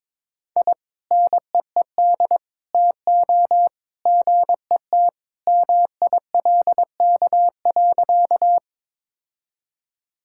Morse code, heard as a keyed tone, about 22 wpm, 700 Hz. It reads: I NEED TO GET MILK.
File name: practice / 22wpm / Sentences with 2-Word Phrases - 22wpm